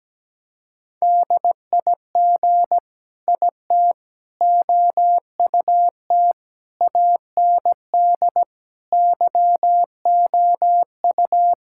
Morse code, heard as a keyed tone, about 17 wpm, 700 Hz. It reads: DIG IT OUT AND YOU
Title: Sentences from Top 1600 Words 17wpm